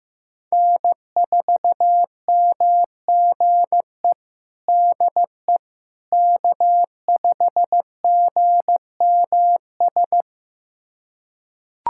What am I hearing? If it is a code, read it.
Morse code, heard as a keyed tone, about 15 wpm, 700 Hz. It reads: N4MGE DE K5GMS
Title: USA Callsign DE Callsign - 5 Characters - 15wpm